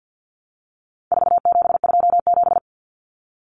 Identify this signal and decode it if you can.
Morse code, heard as a keyed tone, about 48 wpm, 700 Hz. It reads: 4727
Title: Numbers - 4-Digit 50wpm